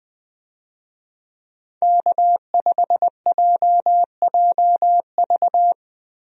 Morse code, heard as a keyed tone, about 20 wpm, 700 Hz. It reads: K5JJV
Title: USA Call Signs - 5 Characters - Repeated 2x - 20wpm